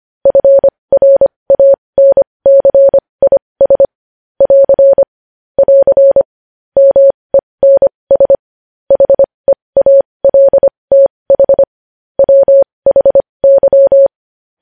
Morse code, heard as a keyed tone, about 25 wpm, spaced as for 20 wpm, 550 Hz. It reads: FRANCIS <AR> <AR> MENS HEALTH WHY
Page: News Headlines in Morse Code at 20 WPM